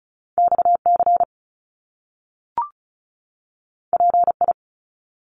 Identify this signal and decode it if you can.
Morse code, heard as a keyed tone, about 36 wpm, 700 Hz. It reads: XC PI
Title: Top Two Letter Combinations in Words 35wpm